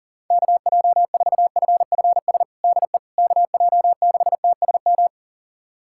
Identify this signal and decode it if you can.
Morse code, heard as a keyed tone, about 37 wpm, 700 Hz. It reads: KJ4FFS DE KJ6TSM
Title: USA Callsign DE Callsign - 6 Characters - 40wpm